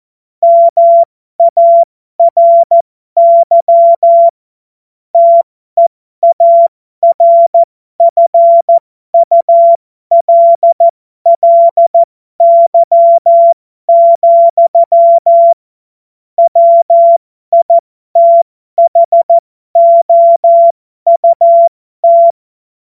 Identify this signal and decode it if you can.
Morse code, heard as a keyed tone, about 14 wpm, spaced as for 12 wpm, 680 Hz. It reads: MARY TEARFULLY, WITHOUT
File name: Morsekurs / Witze